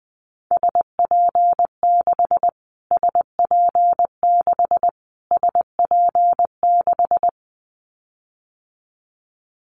Morse code, heard as a keyed tone, about 20 wpm, 700 Hz. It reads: SP6 SP6 SP6
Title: Callsign Prefixes for Top 25 DXCC - Repeated 3x - 20wpm